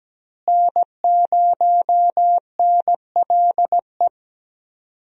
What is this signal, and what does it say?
Morse code, heard as a keyed tone, about 17 wpm, 700 Hz. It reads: N0NLE
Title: USA Call Signs - 5 Characters - Repeated 3x - 17wpm